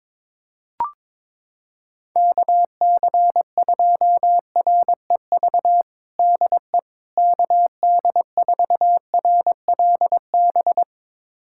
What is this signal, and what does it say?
Morse code, heard as a keyed tone, about 21 wpm, 700 Hz. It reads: KC2REV DE KD4RLB
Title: USA Callsign DE Callsign - 6 Characters - 22wpm